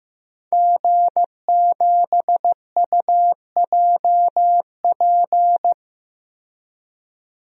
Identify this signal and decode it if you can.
Morse code, heard as a keyed tone, about 16 wpm, 700 Hz. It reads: G7UJP
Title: DXCC-100 Call Signs - 5 Characters - 15wpm